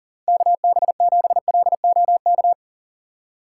Morse code, heard as a keyed tone, about 40 wpm, 700 Hz. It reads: KB7LOK
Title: USA Call Signs - 6 Characters - 4x Character Spacing - 40wpm